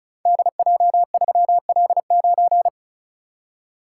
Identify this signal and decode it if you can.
Morse code, heard as a keyed tone, about 36 wpm, 700 Hz. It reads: DJ3L9
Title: Code Groups - 5 Letter:Numbers - 35wpm